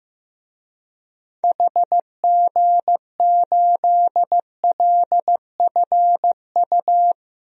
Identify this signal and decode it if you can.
Morse code, heard as a keyed tone, about 15 wpm, 700 Hz. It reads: HG8LFU
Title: DXCC-100 Call Signs - 6 Characters - 15wpm